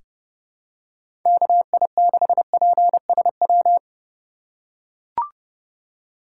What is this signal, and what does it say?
Morse code, heard as a keyed tone, about 32 wpm, 700 Hz. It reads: KI6PSW
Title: USA Call Signs - 3x Character Spacing - 30wpm